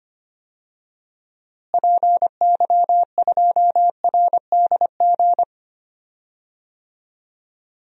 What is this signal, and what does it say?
Morse code, heard as a keyed tone, about 25 wpm, 700 Hz. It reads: PY2RDG
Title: DXCC-100 Call Signs - 6 Characters - Repeated 3x - 25wpm